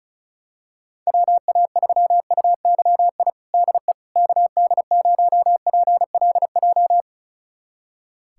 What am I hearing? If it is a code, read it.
Morse code, heard as a keyed tone, about 34 wpm, 700 Hz. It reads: WA3UYI DE KD0PLJ
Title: USA Callsign DE Callsign - 6 Characters - 35wpm